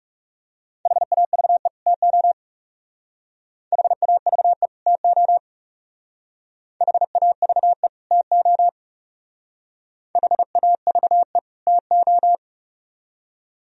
Morse code, heard as a keyed tone, about 37 wpm, 700 Hz. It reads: HAVE TO HAVE TO HAVE TO HAVE TO
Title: Speed-Racing - Sets of 2 Words 30wpm